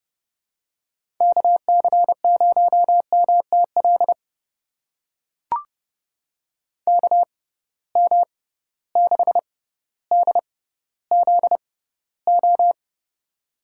Morse code, heard as a keyed tone, about 30 wpm, spaced as for 15 wpm, 700 Hz. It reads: KC0MTL KM6DZO
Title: USA Call Signs - 6 Characters - 6x Character Spacing - 30wpm